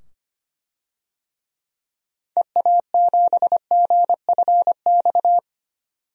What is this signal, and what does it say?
Morse code, heard as a keyed tone, about 25 wpm, 700 Hz. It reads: EA7GFX
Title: DXCC-100 Call Signs - 6 Characters - 25wpm